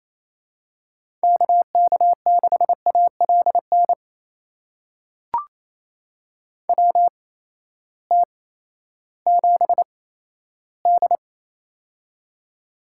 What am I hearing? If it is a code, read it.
Morse code, heard as a keyed tone, about 28 wpm, spaced as for 14 wpm, 700 Hz. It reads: KK6ALN WT7D
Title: USA Call Signs - 8x Character Spacing - 28wpm